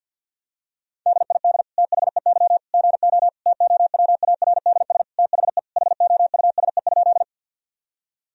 Morse code, heard as a keyed tone, about 49 wpm, 700 Hz. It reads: DID THEY GO TOWARDS THE HOUSE?
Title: Sentences from Top 1000 Words - Review Part 2 - 50wpm